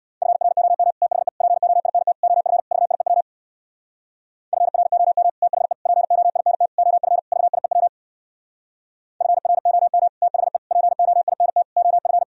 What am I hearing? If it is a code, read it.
Morse code, heard as a keyed tone, about 64 wpm, 700 Hz. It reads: FROM THE POINT OF VIEW FROM THE POINT OF VIEW FROM THE POINT OF